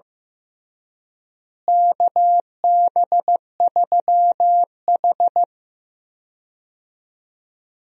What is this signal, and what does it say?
Morse code, heard as a keyed tone, about 16 wpm, 700 Hz. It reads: KB3H
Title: USA Call Signs - 4 Characters - Repeated 3x - 15wpm